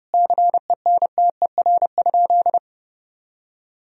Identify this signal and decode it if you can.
Morse code, heard as a keyed tone, about 30 wpm, 700 Hz. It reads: CENTER?
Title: Sentences from Top 1000 Words - Review Part 2 - 30wpm